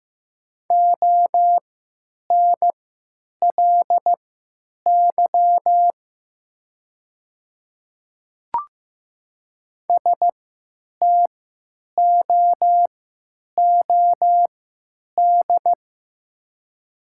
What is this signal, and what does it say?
Morse code, heard as a keyed tone, about 15 wpm, spaced as for 9 wpm, 700 Hz. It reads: ONLY STOOD
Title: 3x Character Spacing - Top 500 Words - 15wpm